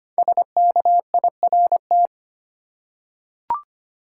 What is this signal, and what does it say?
Morse code, heard as a keyed tone, about 25 wpm, 700 Hz. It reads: SKIRT
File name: 5 Letter Words - First 3 Letters Given - 25wpm